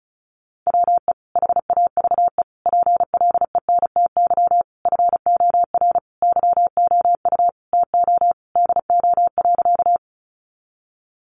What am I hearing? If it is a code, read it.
Morse code, heard as a keyed tone, about 34 wpm, 700 Hz. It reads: WE HAVE PLENTY FOR YOU TO DO.
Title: Sentences from Top 1300 Words 35wpm